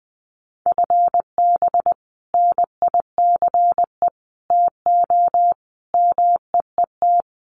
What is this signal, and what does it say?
Morse code, heard as a keyed tone, about 20 wpm, 700 Hz. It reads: FB NICE TO MEET
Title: Common QSO Phrases 20wpm